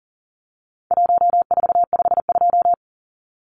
Morse code, heard as a keyed tone, about 40 wpm, 700 Hz. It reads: 1452
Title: Numbers - Contest Style - 4-Digit 40wpm